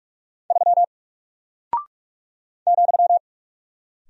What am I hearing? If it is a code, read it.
Morse code, heard as a keyed tone, about 44 wpm, 700 Hz. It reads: 3 ,
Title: Single Character (Letter:Number:Punctuation:Prosign) 45wpm